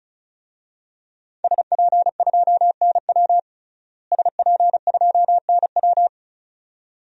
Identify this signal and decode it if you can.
Morse code, heard as a keyed tone, about 36 wpm, 700 Hz. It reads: SP2NW SP2NW
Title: DXCC-100 Call Signs - 5 Characters - Repeated 2x - 35wpm